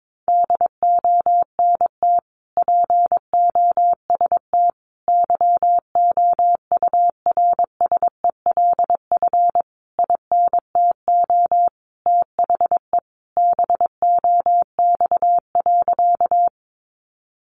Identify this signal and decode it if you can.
Morse code, heard as a keyed tone, about 22 wpm, 700 Hz. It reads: DONT POST YOURSELF INTO THE BOX.
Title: Sentences from Top 1000 Words 22wpm